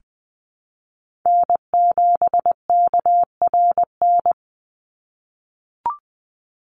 Morse code, heard as a keyed tone, about 20 wpm, 700 Hz. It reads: N7KRN
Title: NA SOTA Chaser Callsigns 20wpm